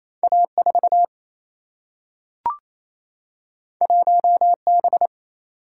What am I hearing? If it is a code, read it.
Morse code, heard as a keyed tone, about 28 wpm, 700 Hz. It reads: A 4 1 B
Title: Single-Letter Single-Number 28wpm